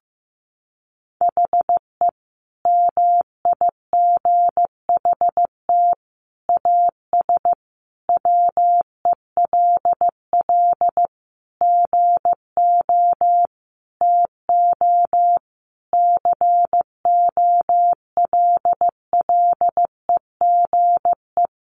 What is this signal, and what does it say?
Morse code, heard as a keyed tone, about 15 wpm, 700 Hz. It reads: HE MIGHT AS WELL GO TO COLLEGE
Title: Sentences from Top 1300 Words 15wpm